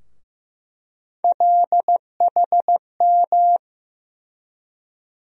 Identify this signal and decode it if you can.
Morse code, heard as a keyed tone, about 15 wpm, 700 Hz. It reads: LHM
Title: USA Call Sign Postfixes - 3 Letters 15wpm